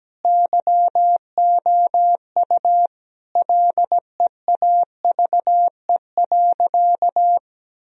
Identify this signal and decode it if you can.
Morse code, heard as a keyed tone, about 17 wpm, 700 Hz. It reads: YOU LEAVE.
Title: Sentences from Top 1000 Words - Review Part 4 - 17wpm